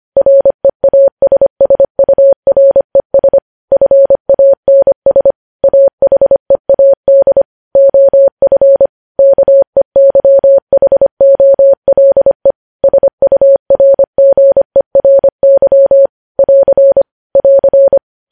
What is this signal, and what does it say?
Morse code, heard as a keyed tone, about 25 wpm, 550 Hz. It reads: REASSURES FANS AHEAD OF KEYHOLE SURGERY <AR> <AR>
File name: podcast / News headlines-1x-25x25